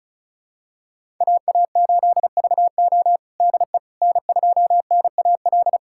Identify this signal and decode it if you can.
Morse code, heard as a keyed tone, about 34 wpm, 700 Hz. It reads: AA8VO DE N2NAL
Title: USA Callsign DE Callsign - 5 Characters - 35wpm